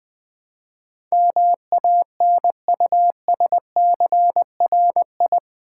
Morse code, heard as a keyed tone, about 20 wpm, 700 Hz. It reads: MANUSCRI
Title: Build A Word - 4 Letters At A Time - 10-Letter Words - 20wpm